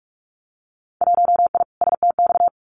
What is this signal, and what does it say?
Morse code, heard as a keyed tone, about 44 wpm, 700 Hz. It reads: 1I STX
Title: Winter Field Day Exchanges 45wpm